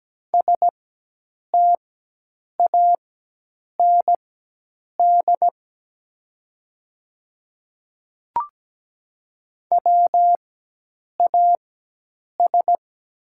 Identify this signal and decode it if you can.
Morse code, heard as a keyed tone, about 17 wpm, spaced as for 8 wpm, 700 Hz. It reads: STAND WAS
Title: 4x Character Spacing - Top 300 Words - 17wpm